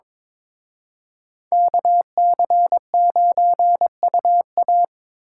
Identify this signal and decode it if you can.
Morse code, heard as a keyed tone, about 25 wpm, 700 Hz. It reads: KC9UA
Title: Speed Racing - USA 2x2 Call Signs 22wpm